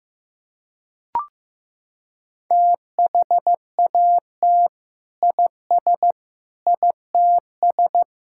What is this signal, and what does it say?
Morse code, heard as a keyed tone, about 15 wpm, 700 Hz. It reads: THAT IS ITS
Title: Sentences from Top 1000 Words - Review Part 5 - 15wpm